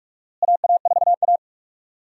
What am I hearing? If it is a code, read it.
Morse code, heard as a keyed tone, about 46 wpm, 700 Hz. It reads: AA4A
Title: USA Call Signs - 4 Characters - Repeated 2x - 45wpm